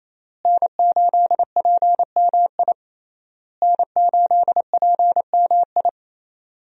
Morse code, heard as a keyed tone, about 27 wpm, 700 Hz. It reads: N8PMI N8PMI
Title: USA Call Signs - 5 Characters - Repeated 3x - 28wpm